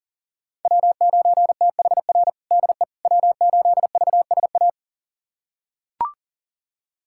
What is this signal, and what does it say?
Morse code, heard as a keyed tone, about 40 wpm, 700 Hz. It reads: W9THR DE W8VSA
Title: USA Callsign DE Callsign - 5 Characters - 40wpm